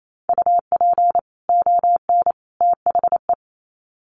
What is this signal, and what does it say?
Morse code, heard as a keyed tone, about 28 wpm, 700 Hz. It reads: UP ON THE
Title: Sets of 3 Words 28wpm